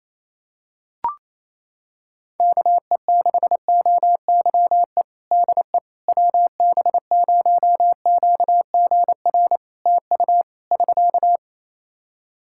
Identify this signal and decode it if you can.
Morse code, heard as a keyed tone, about 28 wpm, 700 Hz. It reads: KE6OYE DE WB0QGR TU <SK>
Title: Practice QSO - Exchange 4 - Ending - 28wpm